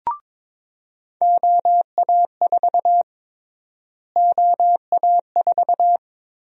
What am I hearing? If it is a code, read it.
Morse code, heard as a keyed tone, about 22 wpm, 700 Hz. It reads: OA4 OA4
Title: USA Call Sign Prefixes - Repeated 2x - 22wpm